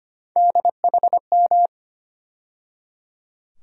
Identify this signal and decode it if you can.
Morse code, heard as a keyed tone, about 25 wpm, 700 Hz. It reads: DHM
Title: USA Call Sign Postfixes - 3 Letters 25wpm